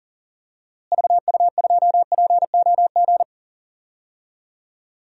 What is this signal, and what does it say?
Morse code, heard as a keyed tone, about 40 wpm, 700 Hz. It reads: VU2POG
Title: DXCC-100 Call Signs - 6 Characters - Repeated 3x - 40wpm